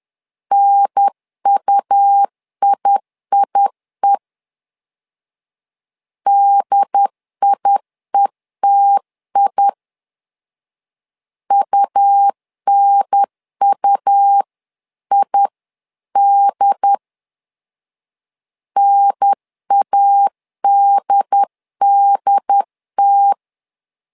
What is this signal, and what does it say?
Morse code, heard as a keyed tone, about 10 wpm, 800 Hz. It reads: NUIIE DIETI UNUID NADDT
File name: LEZIONI / 2 UHDN / 2 UHDN 10wpm 1904